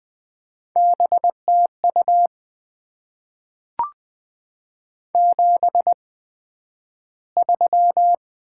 Morse code, heard as a keyed tone, about 20 wpm, spaced as for 11 wpm, 700 Hz. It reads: BTU 73
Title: Top 20 QSO Elements - 8x Character Spacing - 20wpm